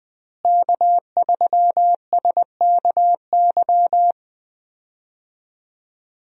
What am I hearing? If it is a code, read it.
Morse code, heard as a keyed tone, about 20 wpm, 700 Hz. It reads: K3SKY
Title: USA Call Signs - 5 Characters - Repeated 2x - 20wpm